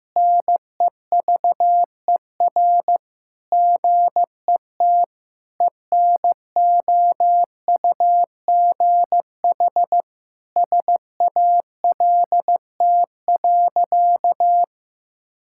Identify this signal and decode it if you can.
Morse code, heard as a keyed tone, about 16 wpm, 700 Hz. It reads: NEVER GET ENOUGH SALT.
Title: Sentences from Top 1000 Words - Review Part 5 - 15wpm